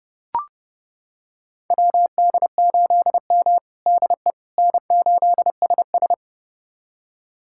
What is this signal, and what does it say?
Morse code, heard as a keyed tone, about 30 wpm, 700 Hz. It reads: WD8M DE N8SS
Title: USA Callsign DE Callsign - 4 Characters - 30wpm